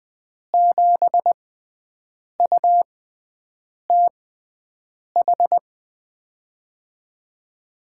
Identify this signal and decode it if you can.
Morse code, heard as a keyed tone, about 20 wpm, spaced as for 7 wpm, 700 Hz. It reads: 7UTH
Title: USA Call Signs - 6 Characters - 6x Character Spacing - 20wpm